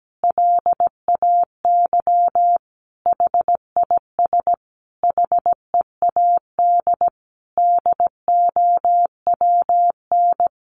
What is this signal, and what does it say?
Morse code, heard as a keyed tone, about 17 wpm, 700 Hz. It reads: LAY HIS HEAD DOWN
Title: Sentences from Top 1000 Words 17wpm